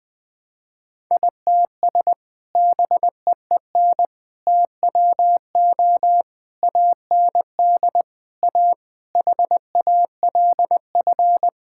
Morse code, heard as a keyed tone, about 20 wpm, 700 Hz. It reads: ITS BEEN TWO AND A HALF
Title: Sentences with 5-Word Phrases - 20wpm